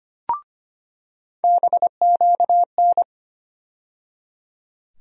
Morse code, heard as a keyed tone, about 25 wpm, 700 Hz. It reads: BQN
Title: USA Call Sign Postfixes - 3 Letters 25wpm